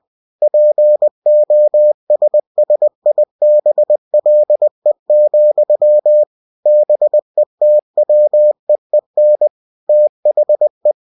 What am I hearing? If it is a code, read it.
Morse code, heard as a keyed tone, about 20 wpm, 600 Hz. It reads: POSSIBLE, BETWEEN THE
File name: cw-books / winnie-the-pooh / winnie-20wpm-page-0024